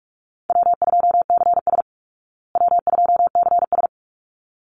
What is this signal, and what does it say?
Morse code, heard as a keyed tone, about 44 wpm, 700 Hz. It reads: W2CS W2CS
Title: Easy Call Signs Repeated 2x - 45wpm